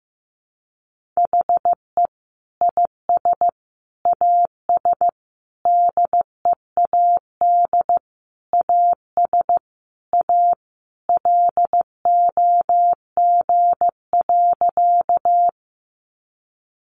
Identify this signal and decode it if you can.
Morse code, heard as a keyed tone, about 14 wpm, 700 Hz. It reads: HE IS AS DEAD AS A LOG.
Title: Sentences from Top 1500 Words 15wpm